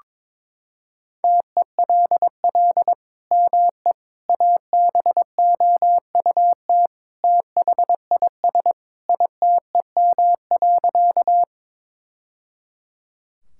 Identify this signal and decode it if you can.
Morse code, heard as a keyed tone, about 22 wpm, 700 Hz. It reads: TELL ME ABOUT THIS ITEM.
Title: Sentences from Top 1000 Words - Review Part 6 - 22wpm